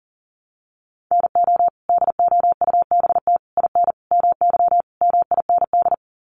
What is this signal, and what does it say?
Morse code, heard as a keyed tone, about 40 wpm, 700 Hz. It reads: NO DOUBT IN MY MIND
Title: Sets of 5 Words Encore2 40wpm